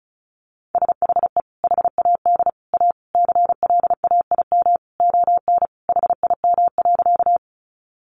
Morse code, heard as a keyed tone, about 34 wpm, 700 Hz. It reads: SHE HAD A CLAIM ON HIM.
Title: Sentences from Top 1500 Words 35wpm